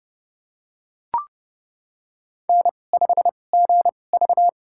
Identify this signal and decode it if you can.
Morse code, heard as a keyed tone, about 30 wpm, spaced as for 22 wpm, 700 Hz. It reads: N5GV
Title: USA Call Signs - 4 Characters - 2x Character Spacing - 30wpm